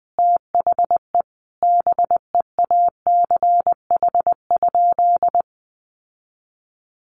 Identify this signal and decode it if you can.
Morse code, heard as a keyed tone, about 20 wpm, 700 Hz. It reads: THE BEACH?
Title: Sentences from Top 1500 Words 20wpm